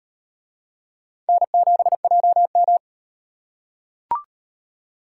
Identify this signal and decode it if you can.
Morse code, heard as a keyed tone, about 38 wpm, 700 Hz. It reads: N7JM